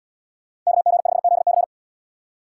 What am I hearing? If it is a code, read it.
Morse code, heard as a keyed tone, about 77 wpm, 700 Hz. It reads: KG5ZC